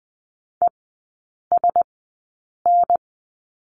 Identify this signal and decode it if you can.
Morse code, heard as a keyed tone, about 19 wpm, spaced as for 8 wpm, 700 Hz. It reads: ESN
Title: 041 Lesson 8e - Number 1 - Three Characters 20wpm